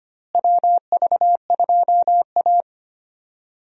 Morse code, heard as a keyed tone, about 25 wpm, 700 Hz. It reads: WV2A
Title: USA Call Signs - 4 Characters - Repeated 2x - 25wpm